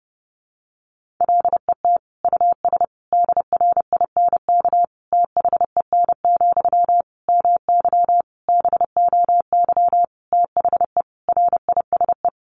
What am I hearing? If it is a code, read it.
Morse code, heard as a keyed tone, about 30 wpm, 700 Hz. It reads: LET US DRINK THEN, MY BOY THE RISE
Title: Sentences from Top 1500 Words 30wpm